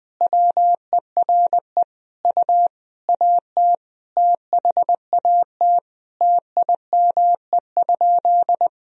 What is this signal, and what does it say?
Morse code, heard as a keyed tone, about 20 wpm, 700 Hz. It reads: WERE U AT THAT TIME?
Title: Sentences from Top 200 Words and Abbreviations 20wpm